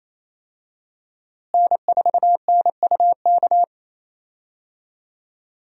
Morse code, heard as a keyed tone, about 28 wpm, 700 Hz. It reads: N4NUK
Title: USA Call Signs - 5 Characters - Repeated 2x - 28wpm